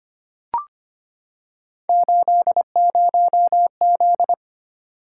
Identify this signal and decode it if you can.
Morse code, heard as a keyed tone, about 25 wpm, 700 Hz. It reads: 80Z
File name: Three Letter-Number Combinations - 25wpm